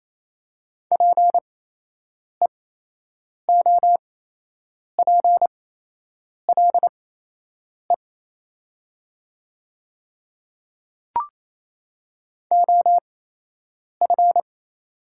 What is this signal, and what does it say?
Morse code, heard as a keyed tone, about 28 wpm, spaced as for 8 wpm, 700 Hz. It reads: PEOPLE OF
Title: 8x Character Spacing - Top 100 Words - 28wpm